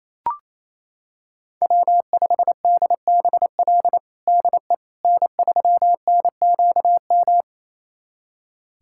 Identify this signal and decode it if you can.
Morse code, heard as a keyed tone, about 28 wpm, 700 Hz. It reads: W5DBL DE N3NQM
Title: USA Callsign DE Callsign - 5 Characters - 28wpm